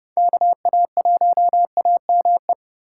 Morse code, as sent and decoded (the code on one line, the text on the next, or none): -.- .- .---- .- -- .
KA1AME